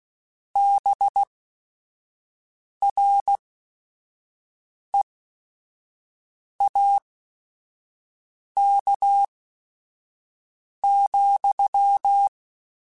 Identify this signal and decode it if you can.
Morse code, heard as a keyed tone, about 16 wpm, spaced as for 5 wpm, 800 Hz. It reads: BREAK,
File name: QSOs / QSOs-mp3s / WA7QOX 1@5